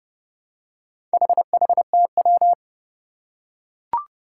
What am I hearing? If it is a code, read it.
Morse code, heard as a keyed tone, about 30 wpm, 700 Hz. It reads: HHTW
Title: Four Challenging Letter-Number Combinations 30wpm